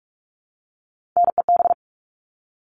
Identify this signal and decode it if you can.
Morse code, heard as a keyed tone, about 44 wpm, 700 Hz. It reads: NEB
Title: All Cut Numbers - 3-Digit 45wpm